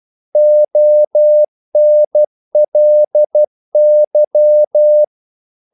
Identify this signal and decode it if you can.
Morse code, heard as a keyed tone, about 12 wpm, 600 Hz. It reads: ONLY
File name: CW / Practice / EnglishQuotations0312